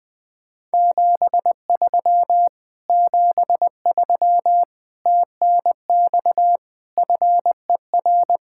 Morse code, heard as a keyed tone, about 20 wpm, 700 Hz. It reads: 73 73 TNX FER